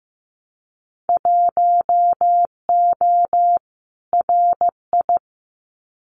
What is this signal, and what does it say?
Morse code, heard as a keyed tone, about 15 wpm, 700 Hz. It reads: 1O RI
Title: Winter Field Day Exchanges 15wpm